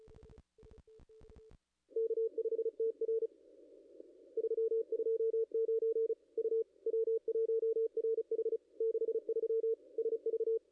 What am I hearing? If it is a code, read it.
Morse code, heard as a keyed tone, about 31 wpm, 450 Hz. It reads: K5TR 329 U W1RH 63 SV
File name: contests / ss-cw-k5tr-op-n5rz -0200